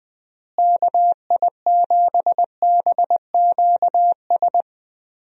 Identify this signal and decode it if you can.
Morse code, heard as a keyed tone, about 20 wpm, 700 Hz. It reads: KI7BQS
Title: USA Call Signs - 6 Characters - Repeated 3x - 20wpm